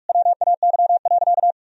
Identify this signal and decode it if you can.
Morse code, heard as a keyed tone, about 43 wpm, 700 Hz. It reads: WAY.
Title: Sentences from Top 1000 Words - Review Part 1 - 45wpm